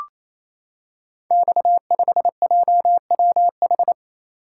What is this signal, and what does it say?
Morse code, heard as a keyed tone, about 29 wpm, 700 Hz. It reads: X5JWH
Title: Code Groups - 5 Letter:Numbers - 28wpm